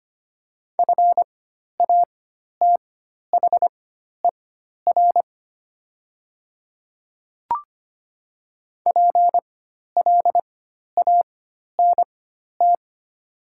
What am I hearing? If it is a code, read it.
Morse code, heard as a keyed tone, about 25 wpm, spaced as for 12 wpm, 700 Hz. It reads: FATHER PLANT
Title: 4x Character Spacing - Top 300 Words - 25wpm